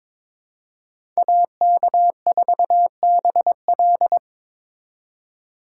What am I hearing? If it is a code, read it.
Morse code, heard as a keyed tone, about 22 wpm, 700 Hz. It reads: AK4BL
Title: USA Call Signs - 5 Characters - 3x Character Spacing - 22wpm